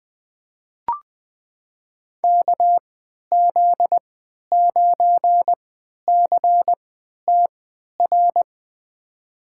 Morse code, heard as a keyed tone, about 20 wpm, spaced as for 11 wpm, 700 Hz. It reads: KZ9CTR
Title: USA Call Signs - 3x Character Spacing - 20wpm